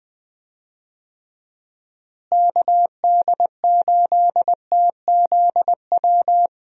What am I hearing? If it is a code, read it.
Morse code, heard as a keyed tone, about 20 wpm, 700 Hz. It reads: KD8TZW
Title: USA Call Signs - 6 Characters - Repeated 3x - 20wpm